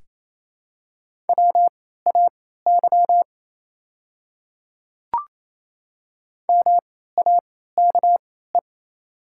Morse code, heard as a keyed tone, about 28 wpm, spaced as for 16 wpm, 700 Hz. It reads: WAY MAKE
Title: 3x Character Spacing - Top 200 Words - 28wpm